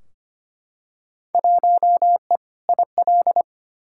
Morse code, heard as a keyed tone, about 25 wpm, 700 Hz. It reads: 1E IL
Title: Field Day Exchanges - 25wpm